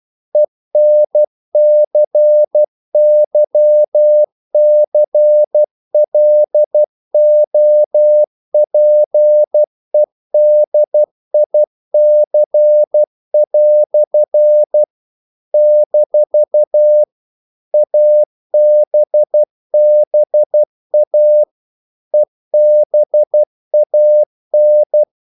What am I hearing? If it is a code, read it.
Morse code, heard as a keyed tone, about 12 wpm, 600 Hz. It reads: ENCYCLOPEDIC" - ABBA EBAN
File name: CW / Practice / EnglishQuotations0112